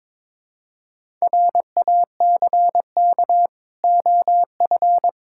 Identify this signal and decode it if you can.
Morse code, heard as a keyed tone, about 22 wpm, 690 Hz. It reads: RACK OF
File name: Difficult Words with Hints - 22wpm